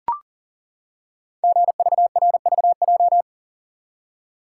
Morse code, heard as a keyed tone, about 40 wpm, 700 Hz. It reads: GVRVJ
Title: Code Groups - 5 Letter:Numbers - 40wpm